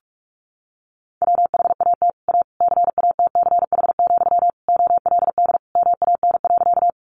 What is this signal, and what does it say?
Morse code, heard as a keyed tone, about 44 wpm, 700 Hz. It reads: WHAT A CATCH, OLD MAN.
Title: Sentences from Top 1500 Words 45wpm